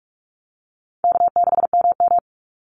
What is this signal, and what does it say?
Morse code, heard as a keyed tone, about 44 wpm, 700 Hz. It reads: K6MM
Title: Easy Call Signs Repeated 3x - 45wpm